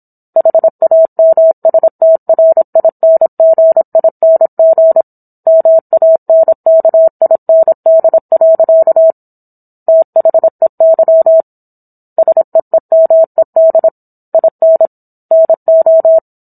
Morse code, heard as a keyed tone, about 27 wpm, 650 Hz. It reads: HAMSTRINGING MANKIND. THEY SEEMED IN NO
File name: Book One / 17-Chapter 17 - 26 WPM